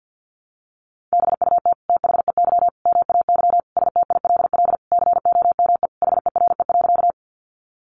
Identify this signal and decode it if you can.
Morse code, heard as a keyed tone, about 50 wpm, 700 Hz. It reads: BUT THEY MAY STILL COME HERE.
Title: Sentences from Top 300 Words 50wpm